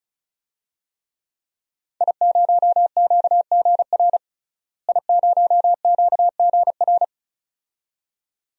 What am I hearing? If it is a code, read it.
Morse code, heard as a keyed tone, about 36 wpm, 700 Hz. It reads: I0QGR I0QGR
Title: DXCC-100 Call Signs - 5 Characters - Repeated 2x - 35wpm